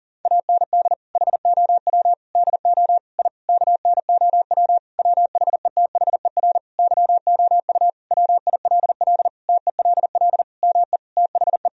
Morse code, heard as a keyed tone, about 37 wpm, 700 Hz. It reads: AND HOW DO I KNOW WHETHER YOU WILL TELL ME THE
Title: Sentences from Top 1000 Words - Review Part 3 - 40wpm